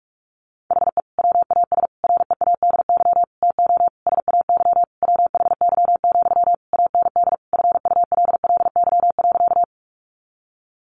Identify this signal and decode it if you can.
Morse code, heard as a keyed tone, about 46 wpm, 700 Hz. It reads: HE WAS READY TO SAY WHY, AND FULLY.
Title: Sentences from Top 1300 Words 45wpm